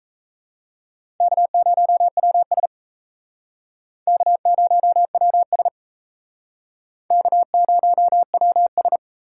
Morse code, heard as a keyed tone, about 35 wpm, 700 Hz. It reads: K0WS K0WS K0WS
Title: Speed Racing - USA 1x2 Call Signs 28wpm